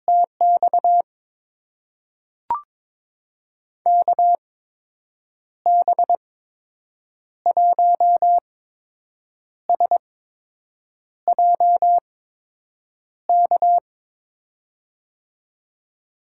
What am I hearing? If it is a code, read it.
Morse code, heard as a keyed tone, about 21 wpm, spaced as for 7 wpm, 700 Hz. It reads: TX KB1SJK
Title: USA Call Signs - 6 Characters - 8x Character Spacing - 22wpm